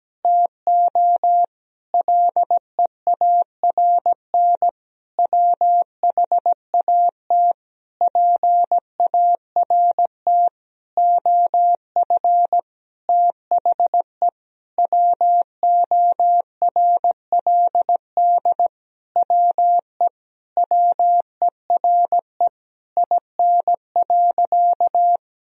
Morse code, heard as a keyed tone, about 17 wpm, 700 Hz. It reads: TO LEARN WHAT PART OF THE WORLD WE WERE IN.
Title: Sentences from Top 1000 Words - Review Part 3 - 17wpm